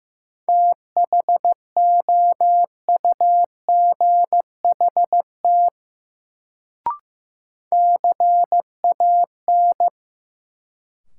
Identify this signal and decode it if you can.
Morse code, heard as a keyed tone, about 15 wpm, 700 Hz. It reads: THOUGHT CAN
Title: Sets of 2 Words - New Word Drill - 15wpm